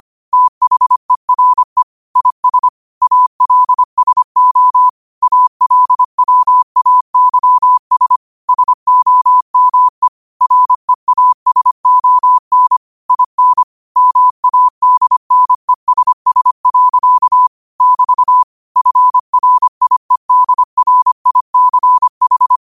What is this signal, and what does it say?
Morse code, heard as a keyed tone, about 25 wpm, 1000 Hz. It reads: THERE IS ALSO ALWAYS SOME REASON IN MADNESS. <BT> FRIEDRICH